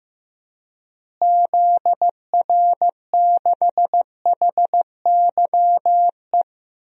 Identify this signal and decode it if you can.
Morse code, heard as a keyed tone, about 15 wpm, 700 Hz. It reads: ZR6HYE
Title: DXCC-100 Call Signs - 6 Characters - Repeated 2x - 15wpm